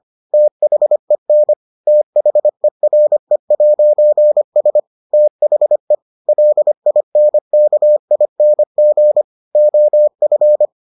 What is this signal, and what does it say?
Morse code, heard as a keyed tone, about 25 wpm, 600 Hz. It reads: THEN THERE'S THE LINKING OF